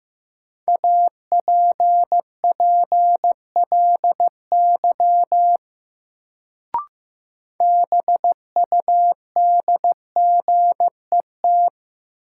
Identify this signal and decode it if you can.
Morse code, heard as a keyed tone, about 15 wpm, 700 Hz. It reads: APPLY BUDGET
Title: Sentences from Top 1200 Words - New Word Drill - 15wpm